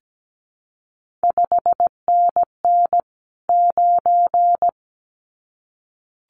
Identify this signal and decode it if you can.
Morse code, heard as a keyed tone, about 17 wpm, 700 Hz. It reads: 5NN 9
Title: CQ WW DX Contest Exchange - 17wpm